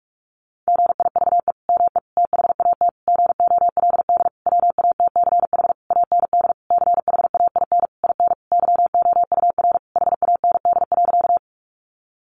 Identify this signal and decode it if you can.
Morse code, heard as a keyed tone, about 44 wpm, 700 Hz. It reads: GIVE ME THAT GOLD WATCH AND CHAIN IN YOUR HAND.
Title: Sentences from Top 1500 Words 45wpm